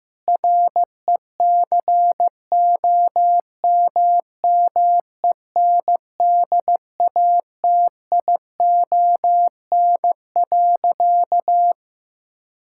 Morse code, heard as a keyed tone, about 16 wpm, 700 Hz. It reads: RECOMMENDATION.
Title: Sentences from Top 1600 Words 15wpm